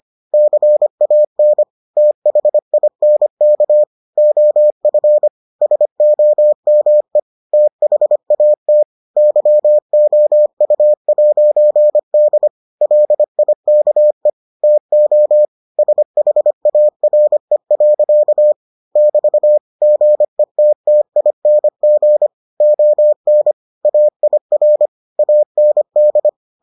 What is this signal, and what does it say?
Morse code, heard as a keyed tone, about 25 wpm, 600 Hz. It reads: CAN THINK OF SOME THAT YOU'D LIKE TO SHARE. <BT> GETTING ON AIR AND